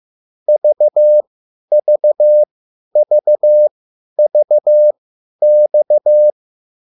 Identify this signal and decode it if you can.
Morse code, heard as a keyed tone, about 15 wpm, spaced as for 10 wpm, 600 Hz. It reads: VVVVX